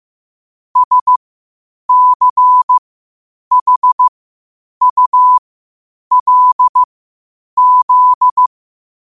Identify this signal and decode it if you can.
Morse code, heard as a keyed tone, about 15 wpm, spaced as for 8 wpm, 1000 Hz. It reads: SCHULZ
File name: qotdfiles / 8wpm